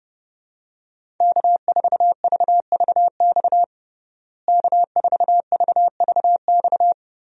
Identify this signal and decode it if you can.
Morse code, heard as a keyed tone, about 30 wpm, 700 Hz. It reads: K4VVX K4VVX
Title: USA Call Signs - 5 Characters - Repeated 3x - 30wpm